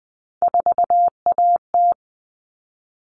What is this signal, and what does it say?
Morse code, heard as a keyed tone, about 20 wpm, 700 Hz. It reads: 4AT
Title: All Cut Numbers - 3-Digit 20wpm